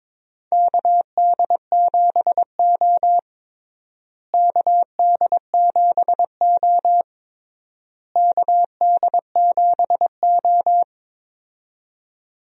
Morse code, heard as a keyed tone, about 22 wpm, 700 Hz. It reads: KD7O KD7O KD7O
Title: USA Call Signs - 4 Characters - Repeated 3x - 22wpm